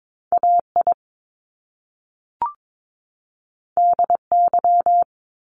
Morse code, heard as a keyed tone, about 22 wpm, 700 Hz. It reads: AI DY
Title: Top Two Letter Combinations in Words 22wpm